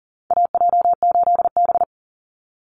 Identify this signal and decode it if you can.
Morse code, heard as a keyed tone, about 40 wpm, 700 Hz. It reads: AJ8B
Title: Call Signs - 40wpm